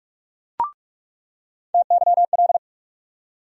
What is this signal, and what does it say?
Morse code, heard as a keyed tone, about 43 wpm, 700 Hz. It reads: TYL
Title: USA Call Sign Postfixes - 3 Letters 45wpm